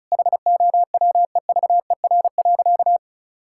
Morse code, heard as a keyed tone, about 36 wpm, 700 Hz. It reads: HOWEVER.
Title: Sentences from Top 1000 Words - Review Part 5 - 35wpm